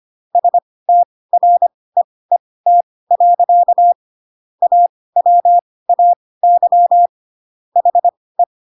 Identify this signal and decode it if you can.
Morse code, heard as a keyed tone, about 25 wpm, spaced as for 18 wpm, 700 Hz. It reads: STREET. AWAY HE
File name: Pinocchio ch3 12 25 ik1hge